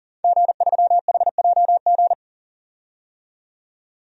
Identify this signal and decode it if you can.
Morse code, heard as a keyed tone, about 40 wpm, 700 Hz. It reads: G3HJG
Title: DXCC-100 Call Signs - 5 Characters - 40wpm